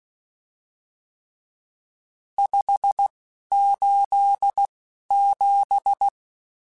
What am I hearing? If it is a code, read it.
Morse code, heard as a keyed tone, about 16 wpm, spaced as for 11 wpm, 800 Hz. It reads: 587
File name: RSTs / RST-mp3s / Various RST @10